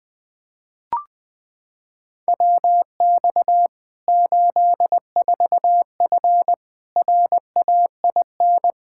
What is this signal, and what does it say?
Morse code, heard as a keyed tone, about 20 wpm, 700 Hz. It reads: WX 84F RAIN
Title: K1USN SST QSO Practice 20wpm